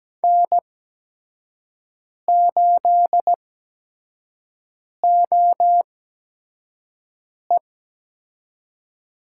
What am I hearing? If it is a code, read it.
Morse code, heard as a keyed tone, about 17 wpm, spaced as for 5 wpm, 700 Hz. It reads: N8OE
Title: USA Call Signs - 5 Characters - 8x Character Spacing - 17wpm